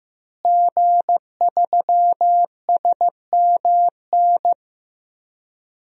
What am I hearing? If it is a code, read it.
Morse code, heard as a keyed tone, about 15 wpm, 700 Hz. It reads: G3SMN
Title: DXCC-100 Call Signs - 5 Characters - Repeated 3x - 15wpm